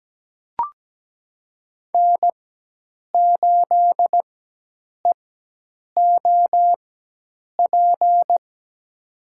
Morse code, heard as a keyed tone, about 17 wpm, spaced as for 8 wpm, 700 Hz. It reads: N8EOP
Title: USA Call Signs - 5 Characters - 4x Character Spacing - 17wpm